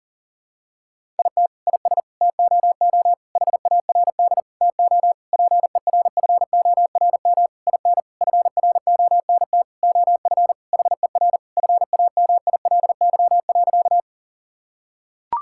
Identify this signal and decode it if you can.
Morse code, heard as a keyed tone, about 40 wpm, 700 Hz. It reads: IT IS TOO HARD TO PERFORM IN FRONT OF HER FAMILY.
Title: Sentences from Top 1000 Words - Review Part 5 - 40wpm